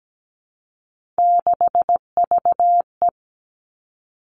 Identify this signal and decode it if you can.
Morse code, heard as a keyed tone, about 17 wpm, 700 Hz. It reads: 6VE
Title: Three Challenging Letter-Number Combinations 17wpm